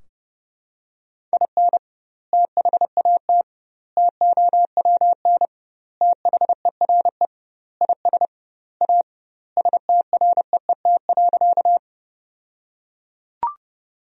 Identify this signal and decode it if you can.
Morse code, heard as a keyed tone, about 30 wpm, 700 Hz. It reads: IN THAT TOWN THERE IS A STREET.
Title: 2x Word-Spacing - Sentences from Top 500 Words - 30wpm